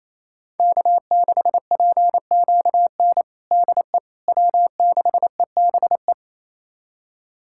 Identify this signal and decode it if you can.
Morse code, heard as a keyed tone, about 27 wpm, 700 Hz. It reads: K6PQN DE W6EBE
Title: USA Callsign DE Callsign - 5 Characters - 28wpm